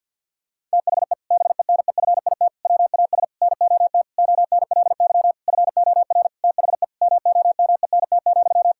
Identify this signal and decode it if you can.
Morse code, heard as a keyed tone, about 50 wpm, 700 Hz. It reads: THE BENEFIT WAS NOT ONLY FOR THE MOMENT,
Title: Sentences from Top 1000 Words - Review Part 6 - 50wpm